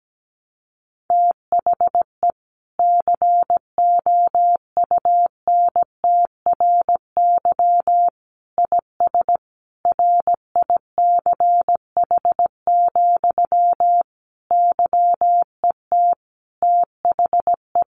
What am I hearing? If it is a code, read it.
Morse code, heard as a keyed tone, about 17 wpm, 700 Hz. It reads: THE COUNTRY IS RICH, YET THE
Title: Sentences from Top 1100 Words 17wpm